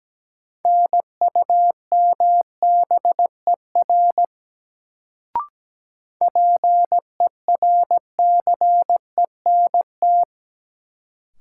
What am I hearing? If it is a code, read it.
Morse code, heard as a keyed tone, about 17 wpm, 700 Hz. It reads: NUMBER PERCENT
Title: Sets of 2 Words - New Word Drill - 17wpm